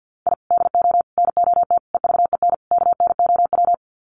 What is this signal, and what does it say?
Morse code, heard as a keyed tone, about 50 wpm, 700 Hz. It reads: I DO NOT EVEN KNOW
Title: Sets of 5 Words 50wpm